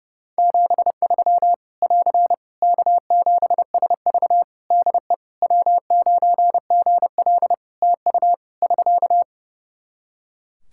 Morse code, heard as a keyed tone, about 30 wpm, 700 Hz. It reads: 73 <AR> K7SV DE W9GL TU <SK>
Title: Practice QSO - Exchange 4 - Ending - 30wpm